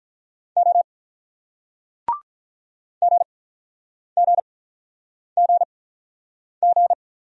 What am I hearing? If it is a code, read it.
Morse code, heard as a keyed tone, about 47 wpm, spaced as for 9 wpm, 700 Hz. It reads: O GGGG